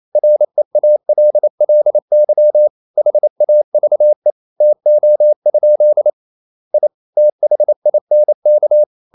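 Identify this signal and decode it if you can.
Morse code, heard as a keyed tone, about 28 wpm, 600 Hz. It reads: REALLY HAVE TO? I THINK
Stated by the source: Morse code practice at 28 wpm, text of Ken Follett's novel Never in English, from CW med Gnister.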